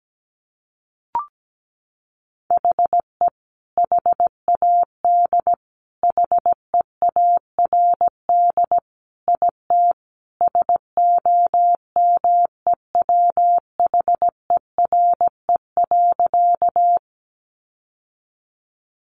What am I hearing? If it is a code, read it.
Morse code, heard as a keyed tone, about 17 wpm, 700 Hz. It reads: HE HAD HEARD IT SOMEWHERE.
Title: Sentences from Top 1400 Words 17wpm